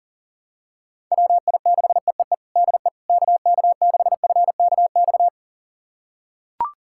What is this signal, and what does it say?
Morse code, heard as a keyed tone, about 40 wpm, 700 Hz. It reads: WI6EEE DE KK6FKX
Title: USA Callsign DE Callsign - 6 Characters - 40wpm